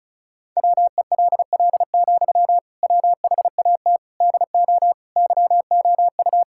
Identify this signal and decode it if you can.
Morse code, heard as a keyed tone, about 35 wpm, 700 Hz. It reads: WELL, WHAT DO YOU
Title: Sentences from Top 1600 Words 35wpm